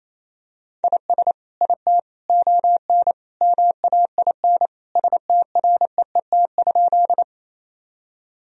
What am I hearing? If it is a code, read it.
Morse code, heard as a keyed tone, about 28 wpm, 700 Hz. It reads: IS IT ON MAIN STREET?
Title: Sentences from Top 1000 Words - Review Part 2 - 28wpm